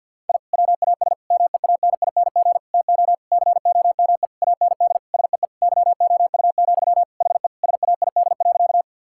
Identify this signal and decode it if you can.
Morse code, heard as a keyed tone, about 50 wpm, 700 Hz. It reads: I WAS MEANING TO COME AND SEE YOU, HE SAID.
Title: Sentences from Top 1000 Words - Review Part 6 - 50wpm